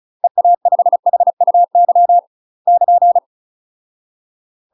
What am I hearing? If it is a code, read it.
Morse code, heard as a keyed tone, about 35 wpm, 700 Hz. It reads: EA5HUY <KN>
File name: UK MASTER QSOs TRAINING / 35WPM / UK MASTER QSOs 0029